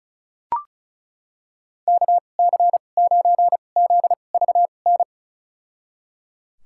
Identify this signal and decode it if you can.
Morse code, heard as a keyed tone, about 36 wpm, spaced as for 25 wpm, 700 Hz. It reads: KC9ZVN
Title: USA Call Signs - 6 Characters - 2x Character Spacing - 35wpm